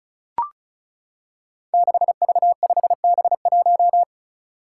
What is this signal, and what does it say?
Morse code, heard as a keyed tone, about 36 wpm, 700 Hz. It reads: 6V5B1
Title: Five Challenging Letter-Number Combinations 35wpm